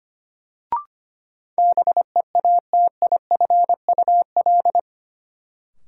Morse code, heard as a keyed tone, about 25 wpm, 700 Hz. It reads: BEATIFUL
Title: Sentences from Top 1300 Words - New Word Drill - 25wpm